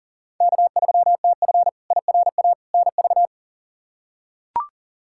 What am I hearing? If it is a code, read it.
Morse code, heard as a keyed tone, about 40 wpm, 700 Hz. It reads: K3TF IRA NV
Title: K1USN SST QSO Practice - 40wpm